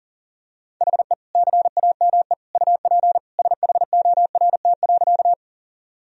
Morse code, heard as a keyed tone, about 40 wpm, 700 Hz. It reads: HE CAME UP SHORT.
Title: Sentences from Top 1000 Words - Review Part 2 - 40wpm